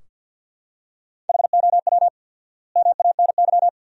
Morse code, heard as a keyed tone, about 47 wpm, 700 Hz. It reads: HOW MANY
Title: 4x Word-Spacing - Sentences from Top 400 Words - 50wpm